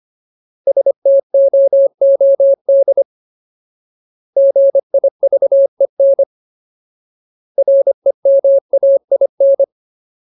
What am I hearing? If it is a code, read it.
Morse code, heard as a keyed tone, about 25 wpm, 550 Hz. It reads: STOOD GIVEN REMAIN
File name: Ultimate Word List / 25WPM Ultimate